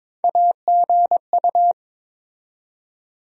Morse code, heard as a keyed tone, about 21 wpm, 700 Hz. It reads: AGU
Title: USA Call Sign Postfixes - 3 Letters 22wpm